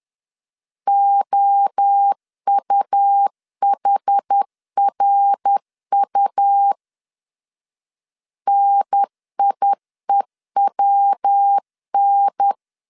Morse code, heard as a keyed tone, about 10 wpm, 800 Hz. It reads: OUHRU NIEWN
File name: CW / DET / 1 3 1500C S B 10wpm 3335